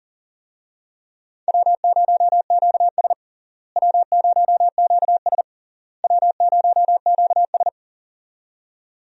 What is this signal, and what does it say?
Morse code, heard as a keyed tone, about 37 wpm, 700 Hz. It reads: W0QS W0QS W0QS
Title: USA Call Signs - 4 Characters - Repeated 3x - 40wpm